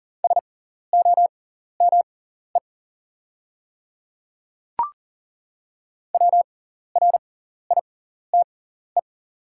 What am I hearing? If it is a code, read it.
Morse code, heard as a keyed tone, about 40 wpm, spaced as for 14 wpm, 690 Hz. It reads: SOME WRITE
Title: 6x Character Spacing - Top 100 Words - 40wpm